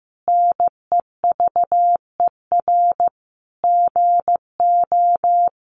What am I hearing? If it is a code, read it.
Morse code, heard as a keyed tone, about 14 wpm, 700 Hz. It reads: NEVER GO
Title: Sentences from Top 300 Words 15wpm